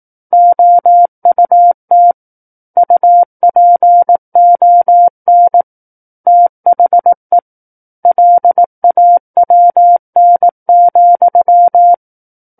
Morse code, heard as a keyed tone, about 18 wpm, 700 Hz. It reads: OUT UPON THE LAWN,
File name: 18 WPM